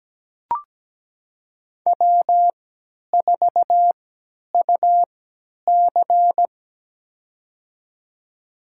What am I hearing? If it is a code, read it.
Morse code, heard as a keyed tone, about 17 wpm, spaced as for 10 wpm, 700 Hz. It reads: W4UC
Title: USA Call Signs - 4 Characters - 3x Character Spacing - 17wpm